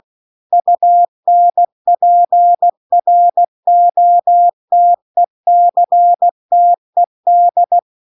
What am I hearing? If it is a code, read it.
Morse code, heard as a keyed tone, about 16 wpm, 700 Hz. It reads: UNPROTECTED
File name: CW-The-Big-Twoheaded-River-2-16WPM